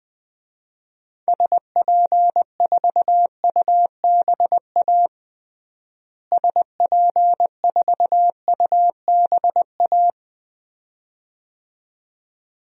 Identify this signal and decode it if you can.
Morse code, heard as a keyed tone, about 21 wpm, 700 Hz. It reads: SP4UBA SP4UBA
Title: DXCC-100 Call Signs - 6 Characters - Repeated 2x - 20wpm